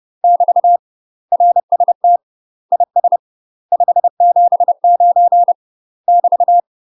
Morse code, heard as a keyed tone, about 30 wpm, 700 Hz. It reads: <BT> RST IS 579 <BT>
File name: UK MASTER QSOs TRAINING / 30WPM / UK MASTER QSOs 0046